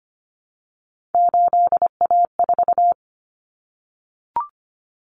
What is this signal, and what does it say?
Morse code, heard as a keyed tone, about 25 wpm, 700 Hz. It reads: 8A4
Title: Common Cut Numbers - 3-Digit 25wpm